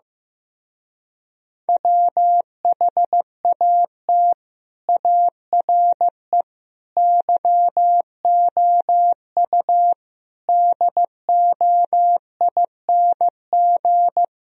Morse code, heard as a keyed tone, about 15 wpm, 700 Hz. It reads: WHAT ARE YOU DOING
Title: Speed-Racing - Sets of 5 Words 15wpm